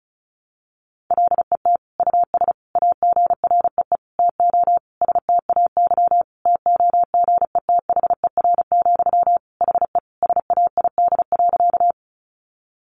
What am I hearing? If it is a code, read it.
Morse code, heard as a keyed tone, about 34 wpm, 700 Hz. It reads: LET US AGREE TO STAY TOGETHER, HE SAID.
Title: Sentences from Top 1100 Words 35wpm